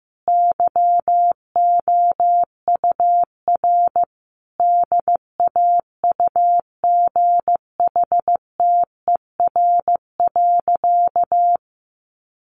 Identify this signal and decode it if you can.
Morse code, heard as a keyed tone, about 15 wpm, 700 Hz. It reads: YOUR DAUGHTER.
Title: Sentences from Top 1500 Words 15wpm